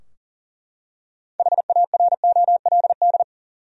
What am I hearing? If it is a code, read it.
Morse code, heard as a keyed tone, about 39 wpm, 700 Hz. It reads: HAROLD
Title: USA Common Names - 40wpm